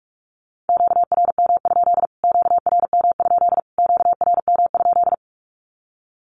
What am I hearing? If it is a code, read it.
Morse code, heard as a keyed tone, about 44 wpm, 700 Hz. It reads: QRM? QRM? QRM?
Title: Q-Codes Repeated 3x 45wpm